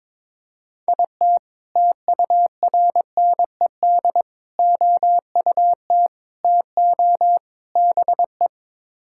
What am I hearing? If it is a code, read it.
Morse code, heard as a keyed tone, about 22 wpm, 700 Hz. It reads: IT TURNED OUT TO BE
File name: practice / 22wpm / Build A Phrase - 5-Word Phrases - 22wpm